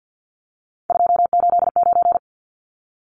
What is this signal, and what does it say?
Morse code, heard as a keyed tone, about 50 wpm, 700 Hz. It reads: 289
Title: Numbers - 3-Digit 50wpm